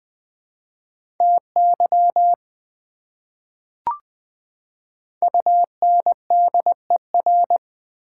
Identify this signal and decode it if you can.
Morse code, heard as a keyed tone, about 20 wpm, 700 Hz. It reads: TY UNDER
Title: Prefixes and Suffixes 20wpm